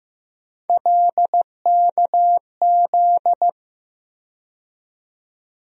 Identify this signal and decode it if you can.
Morse code, heard as a keyed tone, about 15 wpm, 700 Hz. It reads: LKZ
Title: USA Call Sign Postfixes - 3 Letters 15wpm